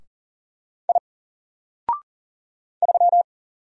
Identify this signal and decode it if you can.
Morse code, heard as a keyed tone, about 40 wpm, 700 Hz. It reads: I 3
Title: Single Character (Letter:Number:Punctuation:Prosign) Alt 40wpm